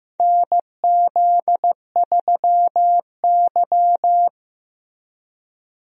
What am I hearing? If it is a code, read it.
Morse code, heard as a keyed tone, about 15 wpm, 700 Hz. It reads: NZ3Y
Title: USA Call Signs - 4 Characters - Repeated 2x - 15wpm